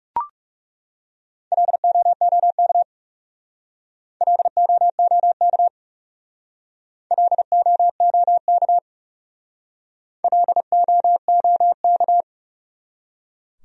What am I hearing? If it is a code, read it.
Morse code, heard as a keyed tone, about 37 wpm, 700 Hz. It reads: LOOK LOOK LOOK LOOK
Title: Speed-Racing Top 200 Words 30wpm